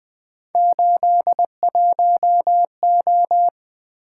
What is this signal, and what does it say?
Morse code, heard as a keyed tone, about 21 wpm, 700 Hz. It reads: 81O
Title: Three Letter-Number Combinations - 20wpm